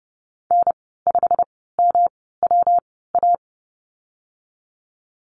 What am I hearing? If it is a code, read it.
Morse code, heard as a keyed tone, about 30 wpm, 700 Hz. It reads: N 5 M W A
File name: Lesson 19g - Letter w - Call Signs 30wpm